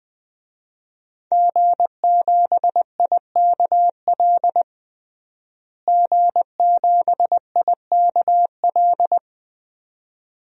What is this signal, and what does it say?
Morse code, heard as a keyed tone, about 20 wpm, 700 Hz. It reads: G7IKL G7IKL
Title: DXCC-100 Call Signs - 5 Characters - Repeated 2x - 20wpm